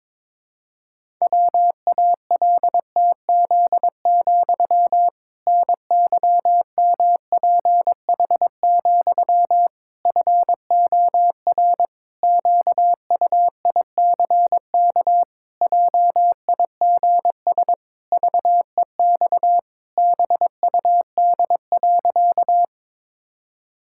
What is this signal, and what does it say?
Morse code, heard as a keyed tone, about 22 wpm, 700 Hz. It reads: WALTZ, NYMPH, FOR QUICK JIGS VEX BUD.
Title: Pangram Challenge v2 - 22wpm